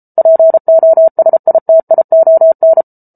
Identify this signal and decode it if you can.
Morse code, heard as a keyed tone, about 34 wpm, 650 Hz. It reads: POSITION